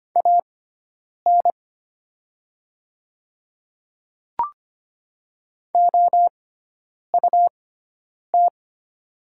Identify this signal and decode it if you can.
Morse code, heard as a keyed tone, about 25 wpm, spaced as for 9 wpm, 700 Hz. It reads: AN OUT
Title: 6x Character Spacing - Top 100 Words - 25wpm